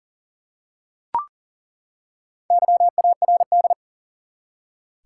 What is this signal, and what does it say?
Morse code, heard as a keyed tone, about 40 wpm, 700 Hz. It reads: YARD
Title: Four Letter Words - 40wpm